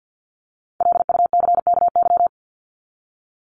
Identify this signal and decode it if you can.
Morse code, heard as a keyed tone, about 50 wpm, 700 Hz. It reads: LUCKY
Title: Top 1751-2000 Words 50wpm